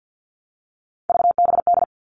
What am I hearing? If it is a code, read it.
Morse code, heard as a keyed tone, about 50 wpm, 700 Hz. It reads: VBD
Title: All Cut Numbers - 3-Digit 50wpm